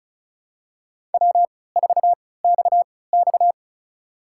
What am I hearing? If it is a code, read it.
Morse code, heard as a keyed tone, about 35 wpm, spaced as for 20 wpm, 700 Hz. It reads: W4XX
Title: USA Call Signs - 4 Characters - 3x Character Spacing - 35wpm